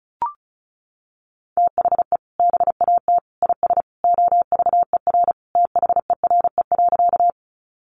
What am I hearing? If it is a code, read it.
Morse code, heard as a keyed tone, about 34 wpm, 700 Hz. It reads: THE BAT IS OVER THERE.
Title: Sentences from Top 1500 Words 35wpm